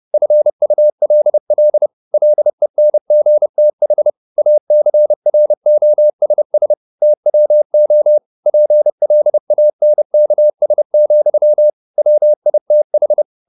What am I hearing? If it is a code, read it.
Morse code, heard as a keyed tone, about 30 wpm, 600 Hz. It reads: FULL LENGTH ACROSS TWO PLANKS, WITH
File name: the old man and the sea / the old man and the sea 1 30wpm